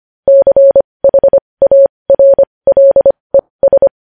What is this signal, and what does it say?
Morse code, heard as a keyed tone, about 25 wpm, spaced as for 20 wpm, 550 Hz. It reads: CHARLES
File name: podcast / News headlines-1x-25x20